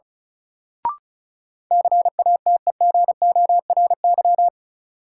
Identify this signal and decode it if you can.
Morse code, heard as a keyed tone, about 36 wpm, 700 Hz. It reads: CATEGORY
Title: Sentences from Top 1000 Words - Review Part 7 - New Word Drill - 35wpm